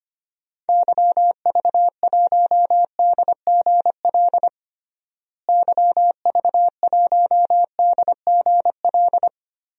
Morse code, heard as a keyed tone, about 25 wpm, 700 Hz. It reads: YV1DGL YV1DGL
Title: DXCC-100 Call Signs - 6 Characters - Repeated 3x - 25wpm